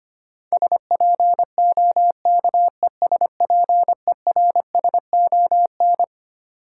Morse code, heard as a keyed tone, about 25 wpm, 700 Hz. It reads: SPOKESPERSON
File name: Twelve Letter Words - 25wpm